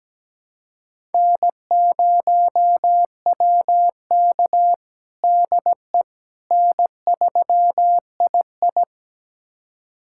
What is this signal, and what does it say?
Morse code, heard as a keyed tone, about 17 wpm, 700 Hz. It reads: N0WK DE N3II
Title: USA Callsign DE Callsign - 4 Characters - 17wpm